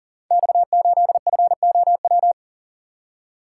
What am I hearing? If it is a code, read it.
Morse code, heard as a keyed tone, about 38 wpm, 700 Hz. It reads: X8FOW
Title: Code Groups - 5 Letter:Numbers - 40wpm